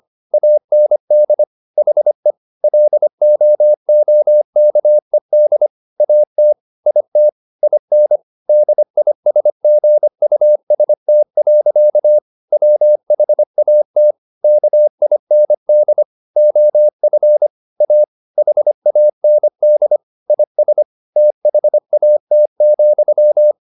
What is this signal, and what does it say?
Morse code, heard as a keyed tone, about 25 wpm, 600 Hz. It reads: AND HE LOOKED AT IT IN DISGUST. WHAT KIND OF A HAND IS THAT,
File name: the old man and the sea part 1 25wpm